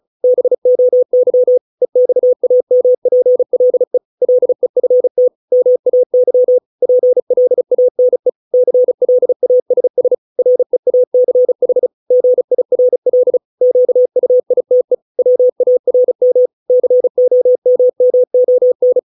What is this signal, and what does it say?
Morse code, heard as a keyed tone, about 34 wpm, 500 Hz. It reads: BOY EXAMPLE LEFT MAY PLANE CLASS REACH GIRL QUITE WARM COMMON